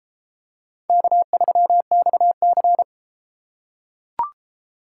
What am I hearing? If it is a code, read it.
Morse code, heard as a keyed tone, about 32 wpm, 700 Hz. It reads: K3XC